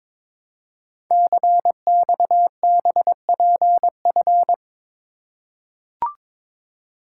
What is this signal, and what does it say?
Morse code, heard as a keyed tone, about 23 wpm, 700 Hz. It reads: CXBPF
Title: Code Groups - 5 Letters - 22wpm